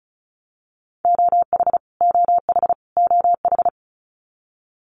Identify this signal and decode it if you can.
Morse code, heard as a keyed tone, about 34 wpm, 700 Hz. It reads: OH OH OH
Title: US States Repeated 3x 35wpm